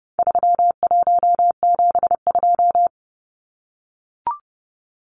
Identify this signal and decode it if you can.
Morse code, heard as a keyed tone, about 30 wpm, 700 Hz. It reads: 3172
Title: Numbers - 4-Digit 30wpm